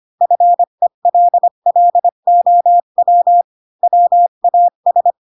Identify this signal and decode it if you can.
Morse code, heard as a keyed tone, about 25 wpm, 700 Hz. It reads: FELLOW WAS
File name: Pinocchio / Pinocchio ch1 20 25 ik1hge